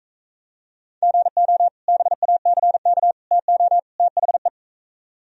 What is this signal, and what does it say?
Morse code, heard as a keyed tone, about 45 wpm, 700 Hz. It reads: GO BACK TO THE